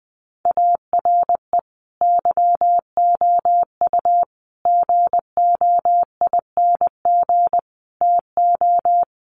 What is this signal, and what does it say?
Morse code, heard as a keyed tone, about 19 wpm, 700 Hz. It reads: ARE YOU GOING TO
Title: Sentences from Top 1200 Words 20wpm